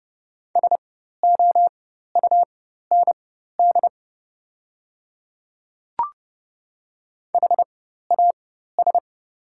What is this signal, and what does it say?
Morse code, heard as a keyed tone, about 30 wpm, spaced as for 14 wpm, 700 Hz. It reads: SOUND HAS
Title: 4x Character Spacing - Top 100 Words - 30wpm